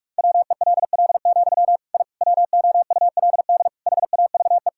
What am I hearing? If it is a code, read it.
Morse code, heard as a keyed tone, about 46 wpm, 700 Hz. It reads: WELL, I WOULD HAVE
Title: Sentences from Top 1000 Words - Review Part 5 - 45wpm